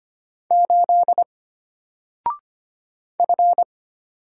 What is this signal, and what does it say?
Morse code, heard as a keyed tone, about 25 wpm, 700 Hz. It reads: 8 F
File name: Single-Letter-Number-25wpm